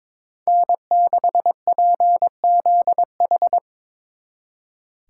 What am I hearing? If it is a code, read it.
Morse code, heard as a keyed tone, about 24 wpm, 700 Hz. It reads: N6PZH
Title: Speed Racing - USA 1x3 Call Signs 22wpm